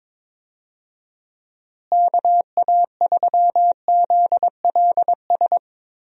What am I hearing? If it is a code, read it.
Morse code, heard as a keyed tone, about 22 wpm, 700 Hz. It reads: KA3ZLS
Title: USA Call Signs - 6 Characters - Repeated 3x - 22wpm